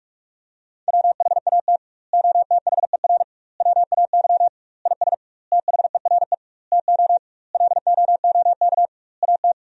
Morse code, heard as a keyed tone, about 46 wpm, 700 Hz. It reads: WHAT OTHER WAY IS THERE TO LOOK AT
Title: 2x Word-Spacing - Sentences from Top 100 Words - 45wpm